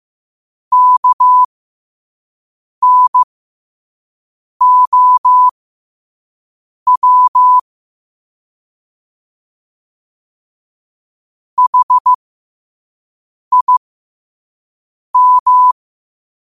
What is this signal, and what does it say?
Morse code, heard as a keyed tone, about 15 wpm, spaced as for 5 wpm, 1000 Hz. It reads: KNOW HIM
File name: qotdfiles / 5wpm